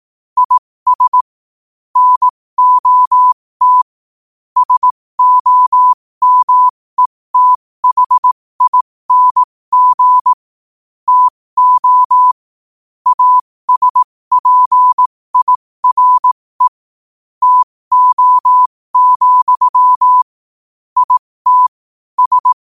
Morse code, heard as a keyed tone, about 18 wpm, spaced as for 15 wpm, 1000 Hz. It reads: IS NOT SOMETHING TO ASPIRE TO, IT S
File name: qotdfiles / 15wpm